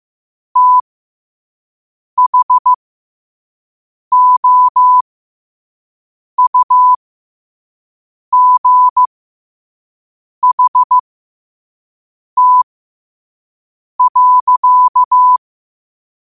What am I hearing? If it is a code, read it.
Morse code, heard as a keyed tone, about 15 wpm, spaced as for 5 wpm, 1000 Hz. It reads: THOUGHT.
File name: qotdfiles / 5wpm